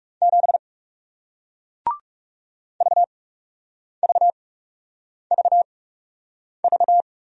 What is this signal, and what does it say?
Morse code, heard as a keyed tone, about 40 wpm, spaced as for 8 wpm, 700 Hz. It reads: 7 VVVV